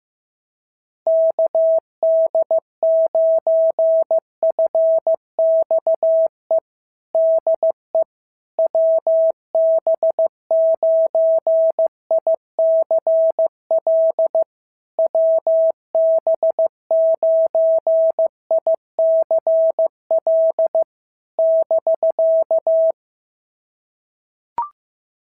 Morse code, heard as a keyed tone, about 15 wpm, 650 Hz. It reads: KD9FXE DE WB9ICL WB9ICL <BK>
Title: SOTA QSO Practice 15wpm